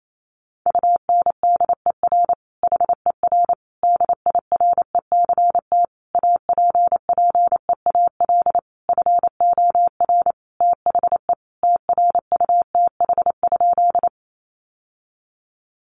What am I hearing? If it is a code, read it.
Morse code, heard as a keyed tone, about 28 wpm, 700 Hz. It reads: UNDER HER DIRECT APPEAL FOR THE TRUTH?
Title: Sentences from Top 1300 Words 28wpm